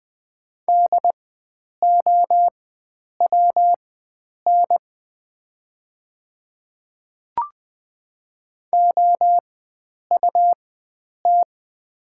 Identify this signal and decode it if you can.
Morse code, heard as a keyed tone, about 20 wpm, spaced as for 9 wpm, 700 Hz. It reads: DOWN OUT
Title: 4x Character Spacing - Top 100 Words - 20wpm